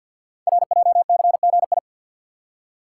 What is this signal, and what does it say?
Morse code, heard as a keyed tone, about 50 wpm, 700 Hz. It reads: RJCGI
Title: Code Groups - 5 Letters - 50wpm